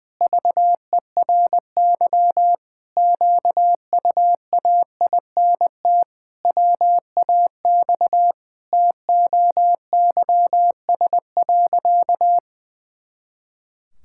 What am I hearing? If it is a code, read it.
Morse code, heard as a keyed tone, about 21 wpm, 700 Hz. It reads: VERY QUAINT WAX TOYS.
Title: Pangram Challenge v2 - 20wpm